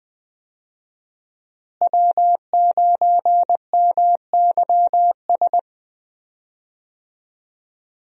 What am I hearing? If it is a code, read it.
Morse code, heard as a keyed tone, about 20 wpm, 700 Hz. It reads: W9MYS
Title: USA Call Signs - 5 Characters - Repeated 2x - 20wpm